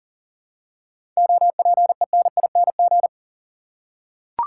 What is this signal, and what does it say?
Morse code, heard as a keyed tone, about 40 wpm, 700 Hz. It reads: OPENING
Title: Seven Letter Words - 40wpm